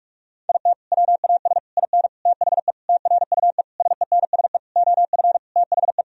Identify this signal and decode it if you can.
Morse code, heard as a keyed tone, about 41 wpm, 700 Hz. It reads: IT WAS IN THE TRUE SENSE OF THE
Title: Sentences from Top 1000 Words - Review Part 4 - 45wpm